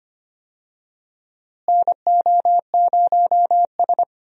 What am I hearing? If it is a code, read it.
Morse code, heard as a keyed tone, about 25 wpm, 700 Hz. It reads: NO0S
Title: USA Call Signs - 4 Characters - Repeated 3x - 25wpm